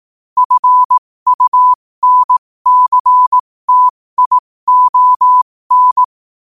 Morse code, heard as a keyed tone, about 18 wpm, spaced as for 15 wpm, 1000 Hz. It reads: FUNCTION
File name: qotdfiles / 15wpm